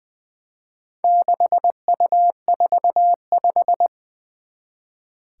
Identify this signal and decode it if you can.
Morse code, heard as a keyed tone, about 20 wpm, 700 Hz. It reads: 6U45
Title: Four Challenging Letter-Number Combinations 20wpm